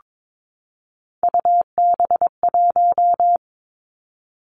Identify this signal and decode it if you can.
Morse code, heard as a keyed tone, about 22 wpm, 700 Hz. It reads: UB1
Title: Three Challenging Letter-Number Combinations 22wpm